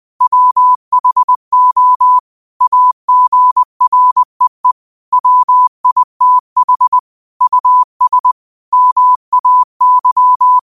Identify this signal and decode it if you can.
Morse code, heard as a keyed tone, about 20 wpm, 1000 Hz. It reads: WHO AGREE WITH US MAY
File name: qotdfiles / 20wpm